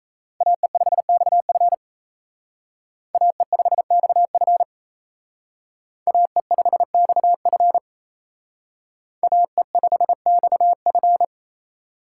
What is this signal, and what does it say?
Morse code, heard as a keyed tone, about 32 wpm, 700 Hz. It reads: AE5XF AE5XF AE5XF AE5XF
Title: Speed Racing - USA Call Signs 28wpm